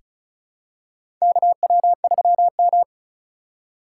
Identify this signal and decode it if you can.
Morse code, heard as a keyed tone, about 34 wpm, 700 Hz. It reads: KW3M
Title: USA Call Signs - 4 Characters - 4x Character Spacing - 35wpm